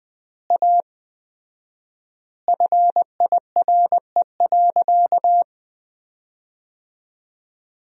Morse code, heard as a keyed tone, about 20 wpm, 700 Hz. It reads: A FIRE.
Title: 4x Word-Spacing - Sentences from Top 400 Words - 20wpm